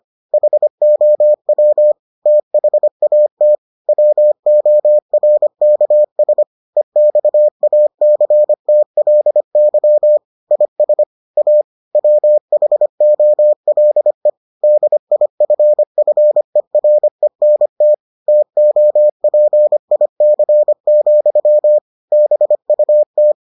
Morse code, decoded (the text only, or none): HOW THAT WORKS EXACTLY IS A WHOLE DIFFERENT TOPIC, BUT